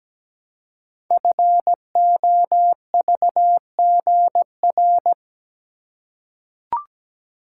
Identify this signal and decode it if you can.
Morse code, heard as a keyed tone, about 17 wpm, 700 Hz. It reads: FOVGR
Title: Code Groups - 5 Letters - 17wpm